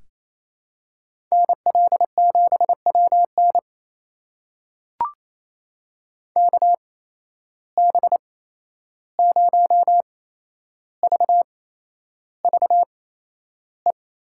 Morse code, heard as a keyed tone, about 28 wpm, spaced as for 11 wpm, 700 Hz. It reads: NL7WN KB0VVE
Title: USA Call Signs - 8x Character Spacing - 28wpm